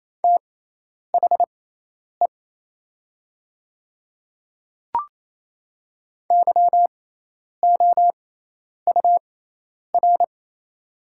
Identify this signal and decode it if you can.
Morse code, heard as a keyed tone, about 28 wpm, spaced as for 10 wpm, 700 Hz. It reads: THE YOUR
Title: 6x Character Spacing - Top 100 Words - 28wpm